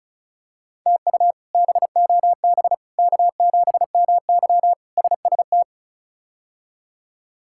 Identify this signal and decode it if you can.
Morse code, heard as a keyed tone, about 34 wpm, 700 Hz. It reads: TU BOB K7MY SST
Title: K1USN SST QSO Practice - 35wpm